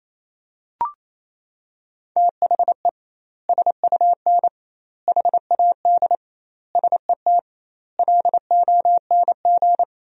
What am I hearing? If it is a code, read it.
Morse code, heard as a keyed tone, about 28 wpm, 700 Hz. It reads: THE SUN HAD SET LONG
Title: 2x Word-Spacing - Sentences from Top 300 Words - 28wpm